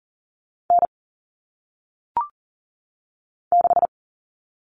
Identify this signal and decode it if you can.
Morse code, heard as a keyed tone, about 39 wpm, 700 Hz. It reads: N 6
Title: All Cut Numbers - 1-Digit 40wpm